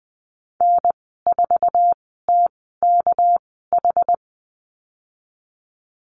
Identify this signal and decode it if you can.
Morse code, heard as a keyed tone, about 20 wpm, spaced as for 14 wpm, 700 Hz. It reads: N4TKH
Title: 238 Lesson 38g - Letter q - Call Signs 20wpm